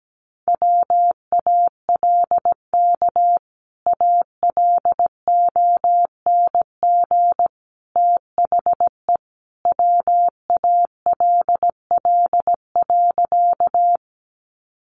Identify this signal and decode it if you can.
Morse code, heard as a keyed tone, about 17 wpm, 700 Hz. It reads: WALK ALONG THE WALL.
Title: Sentences from Top 1100 Words 17wpm